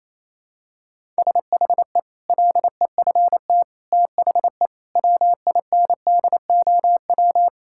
Morse code, heard as a keyed tone, about 28 wpm, 700 Hz. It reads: SHE LEFT THE WINDOW
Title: Sentences from Top 1600 Words 28wpm